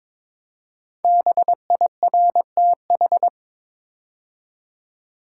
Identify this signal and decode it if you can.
Morse code, heard as a keyed tone, about 22 wpm, 700 Hz. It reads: BIRTH
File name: 5 Letter Words - First 3 Letters Given - 22wpm